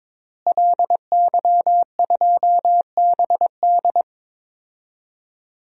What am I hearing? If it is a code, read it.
Morse code, heard as a keyed tone, about 21 wpm, 700 Hz. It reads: LY2BD
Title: DXCC-100 Call Signs - 5 Characters - Repeated 3x - 22wpm